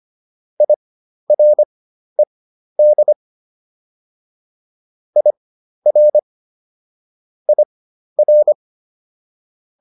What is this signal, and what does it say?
Morse code, heard as a keyed tone, about 25 wpm, spaced as for 12 wpm, 600 Hz. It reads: IRED IR IR
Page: fundamental-practice-files